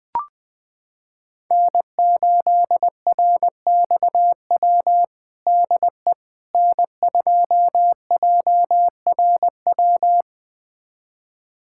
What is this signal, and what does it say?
Morse code, heard as a keyed tone, about 20 wpm, 700 Hz. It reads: N8RXW DE N2JRW
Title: USA Callsign DE Callsign - 5 Characters - 20wpm